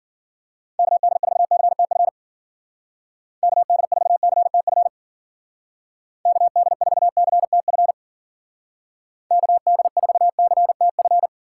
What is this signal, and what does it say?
Morse code, heard as a keyed tone, about 51 wpm, 700 Hz. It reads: KD4CTF KD4CTF KD4CTF KD4CTF